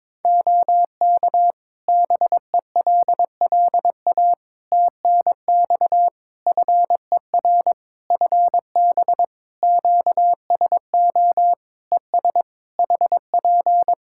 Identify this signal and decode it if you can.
Morse code, heard as a keyed tone, about 22 wpm, 700 Hz. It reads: OK BELLA TNX FER FB QSO ES HP
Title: K1USN SST QSO Practice 22wpm